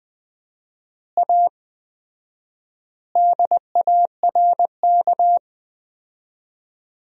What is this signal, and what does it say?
Morse code, heard as a keyed tone, about 21 wpm, 700 Hz. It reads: A DARK
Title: 4x Word-Spacing - Sentences from Top 500 Words - 20wpm